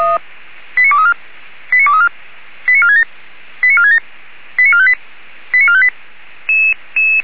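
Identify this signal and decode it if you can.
Five-tone selective call (ZVEI), heard as quick runs of five tones, tones of about 70 ms. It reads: 89234 89234 89467 89467 89458 89458
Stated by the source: ZVEI five-tone selective call 89338 received by SDR.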